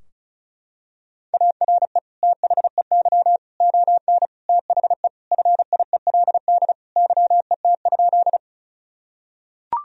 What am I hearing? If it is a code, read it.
Morse code, heard as a keyed tone, about 36 wpm, 700 Hz. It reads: ARE THEY ON THE FIELD YET?
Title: Sentences from Top 1000 Words - Review Part 2 - 35wpm